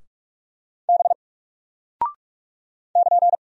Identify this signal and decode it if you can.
Morse code, heard as a keyed tone, about 44 wpm, 700 Hz. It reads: B <KN>
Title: Single Character (Letter:Number:Punctuation:Prosign) Alt 45wpm